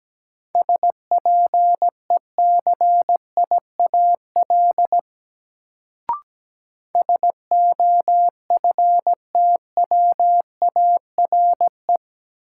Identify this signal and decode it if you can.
Morse code, heard as a keyed tone, about 17 wpm, 700 Hz. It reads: SPECIAL SOFTWARE
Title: Sentences from Top 1000 Words - Review Part 3 - New Word Drill - 17wpm